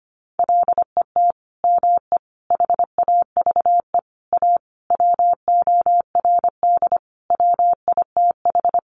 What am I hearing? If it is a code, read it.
Morse code, heard as a keyed tone, about 25 wpm, 700 Hz. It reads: LET ME HAVE A WORD WITH
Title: Sentences from Top 300 Words 25wpm